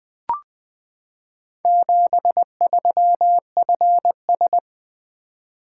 Morse code, heard as a keyed tone, about 20 wpm, 700 Hz. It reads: 73FS
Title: Four Letter-Number Combinations - 20wpm